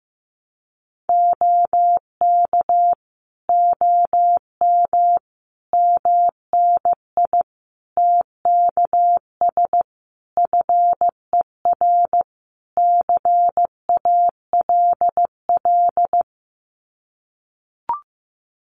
Morse code, heard as a keyed tone, about 14 wpm, 700 Hz. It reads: OK OM MNI TKS FER CALL
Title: Common QSO Phrases 15wpm